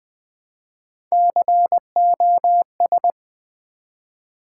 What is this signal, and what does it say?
Morse code, heard as a keyed tone, about 20 wpm, 700 Hz. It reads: COS
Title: Build A Word - 1 Letter At A Time - 4-Letter Words - 20wpm